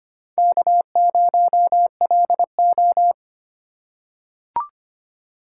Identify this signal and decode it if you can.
Morse code, heard as a keyed tone, about 25 wpm, 700 Hz. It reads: K0LO
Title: USA Call Signs - 2x Character Spacing - 25wpm